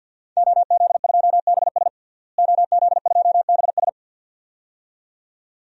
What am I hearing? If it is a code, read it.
Morse code, heard as a keyed tone, about 46 wpm, 700 Hz. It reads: OZ2BS OZ2BS
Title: DXCC-100 Call Signs - 5 Characters - Repeated 2x - 50wpm